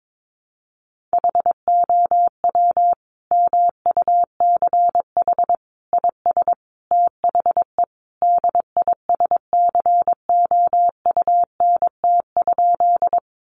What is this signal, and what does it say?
Morse code, heard as a keyed tone, about 22 wpm, 700 Hz. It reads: HOW MUCH IS THE DISCOUNT?
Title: Sentences from Top 1200 Words 22wpm